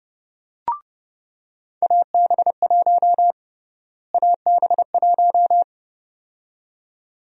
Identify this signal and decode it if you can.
Morse code, heard as a keyed tone, about 30 wpm, 700 Hz. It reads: AB1 AB1
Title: USA Call Sign Prefixes - Repeated 2x - 30wpm